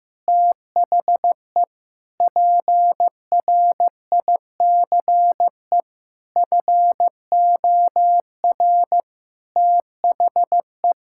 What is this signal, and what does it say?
Morse code, heard as a keyed tone, about 15 wpm, 700 Hz. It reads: THE PRICE FOR THE
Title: Sentences from Top 1000 Words - Review Part 6 - 15wpm